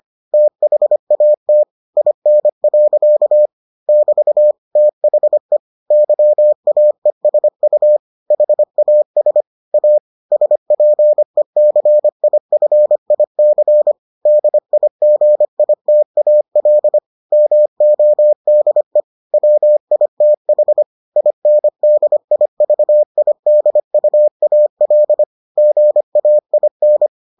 Morse code, heard as a keyed tone, about 25 wpm, 600 Hz. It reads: THAT IN. <BT> THE YAESU HAS A SPECIFIC DIGITAL MODE WITH INDIVIDUAL GAIN